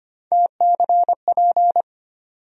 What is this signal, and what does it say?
Morse code, heard as a keyed tone, about 25 wpm, 700 Hz. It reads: TCP
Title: USA Call Sign Postfixes - 3 Letters 25wpm